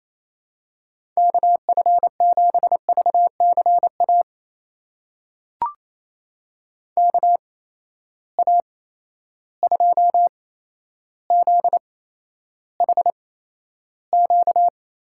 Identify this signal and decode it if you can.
Morse code, heard as a keyed tone, about 28 wpm, spaced as for 12 wpm, 700 Hz. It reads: KF7VCA KA2ZHQ
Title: USA Call Signs - 6 Characters - 8x Character Spacing - 28wpm